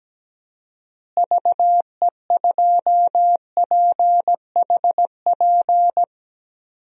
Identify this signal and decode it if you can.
Morse code, heard as a keyed tone, about 17 wpm, 700 Hz. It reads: VE2PHP
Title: DXCC-100 Call Signs - 6 Characters - Repeated 2x - 17wpm